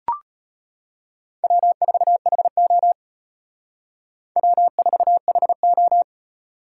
Morse code, heard as a keyed tone, about 32 wpm, 700 Hz. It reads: W4HO W4HO